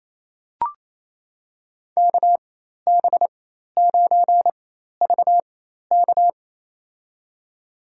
Morse code, heard as a keyed tone, about 28 wpm, spaced as for 13 wpm, 700 Hz. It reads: KB9VK
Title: USA Call Signs - 5 Characters - 4x Character Spacing - 28wpm